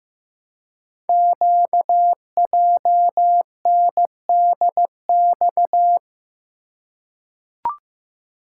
Morse code, heard as a keyed tone, about 15 wpm, 700 Hz. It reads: QJNDX
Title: Code Groups - 5 Letters - 15wpm